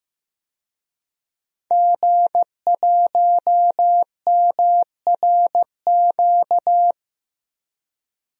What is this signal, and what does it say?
Morse code, heard as a keyed tone, about 15 wpm, 700 Hz. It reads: G1MRQ
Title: DXCC-100 Call Signs - 5 Characters - Repeated 2x - 15wpm